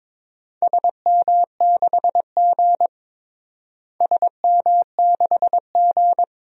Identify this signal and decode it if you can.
Morse code, heard as a keyed tone, about 22 wpm, 700 Hz. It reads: SM6G SM6G
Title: DXCC-100 Call Signs - 4 Characters - Repeated 3x - 22wpm